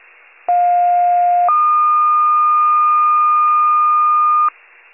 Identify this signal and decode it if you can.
Two-tone page, heard as a one-second tone, then a three-second tone, about 710 then 1220 Hz. It is heard from 0.5 to 4.5 s.